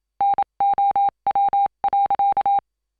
Morse code, heard as a keyed tone, about 27 wpm, 800 Hz. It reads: NOW.